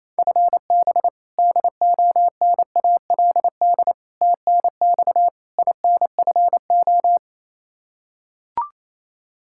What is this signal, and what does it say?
Morse code, heard as a keyed tone, about 27 wpm, 700 Hz. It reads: FB DONALD TNX INFO
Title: Practice QSO - Exchange 3 - More Info - 28wpm